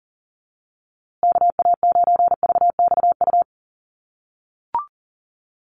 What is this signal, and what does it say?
Morse code, heard as a keyed tone, about 40 wpm, 700 Hz. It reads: KA9VXU
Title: USA Call Signs Repeated 2x 40wpm